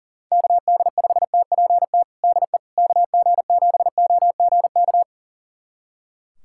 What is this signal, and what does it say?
Morse code, heard as a keyed tone, about 40 wpm, 700 Hz. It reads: KD5TPT DE KG7OGK
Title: USA Callsign DE Callsign 40wpm